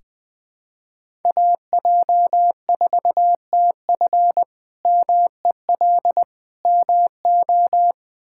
Morse code, heard as a keyed tone, about 20 wpm, 700 Hz. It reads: AJ4TF MEL MO
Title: K1USN SST QSO Practice - 20wpm